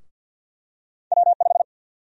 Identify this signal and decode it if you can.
Morse code, heard as a keyed tone, about 52 wpm, 700 Hz. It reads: W5
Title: USA Call Sign Prefixes - Repeated 2x - 50wpm